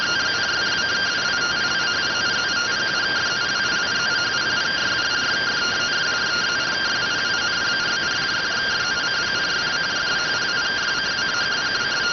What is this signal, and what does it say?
Two-tone teleprinter signal, about 170 Hz shift.